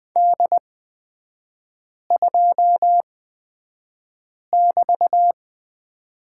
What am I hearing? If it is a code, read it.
Morse code, heard as a keyed tone, about 20 wpm, spaced as for 5 wpm, 700 Hz. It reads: D2<BT>
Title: 224 Lesson 36e - BT - Three Characters 20wpm